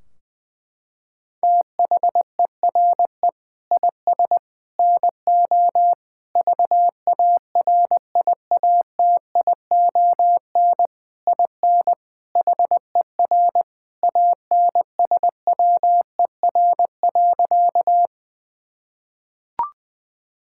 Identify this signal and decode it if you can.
Morse code, heard as a keyed tone, about 21 wpm, 700 Hz. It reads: THERE IS NO VARIATION IN HER ANSWER.
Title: Sentences from Top 1600 Words 20wpm